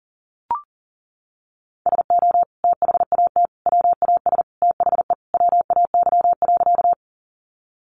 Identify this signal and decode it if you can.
Morse code, heard as a keyed tone, about 40 wpm, 700 Hz. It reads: SO THAT WAS THE WAY.
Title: Sentences from Top 100 Words 40wpm